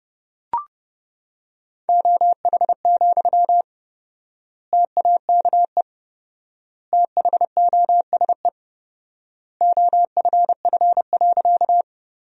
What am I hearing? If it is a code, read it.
Morse code, heard as a keyed tone, about 30 wpm, 700 Hz. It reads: OH, TAKE THOSE OFF.
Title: 4x Word-Spacing - Sentences from Top 500 Words - 30wpm